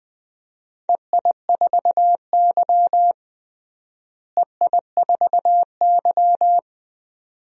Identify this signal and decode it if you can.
Morse code, heard as a keyed tone, about 20 wpm, 700 Hz. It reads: EI4Y EI4Y
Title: DXCC-100 Call Signs - 4 Characters - Repeated 2x - 20wpm